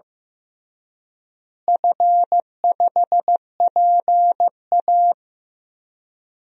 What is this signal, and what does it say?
Morse code, heard as a keyed tone, about 15 wpm, 700 Hz. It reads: F5PA
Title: DXCC-100 Call Signs - 4 Characters - Repeated 2x - 15wpm